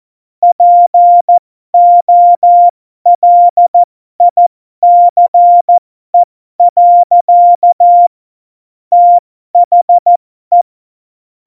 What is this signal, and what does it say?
Morse code, heard as a keyed tone, about 14 wpm, spaced as for 12 wpm, 700 Hz. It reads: POLICE. THE
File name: Morsekurs / Witze